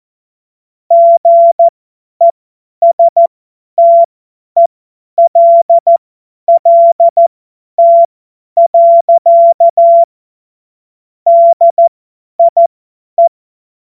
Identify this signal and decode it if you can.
Morse code, heard as a keyed tone, about 14 wpm, spaced as for 10 wpm, 680 Hz. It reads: GESTELLT. DIE
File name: Morsekurs / Witze / Witz 14. März 2023 50 F70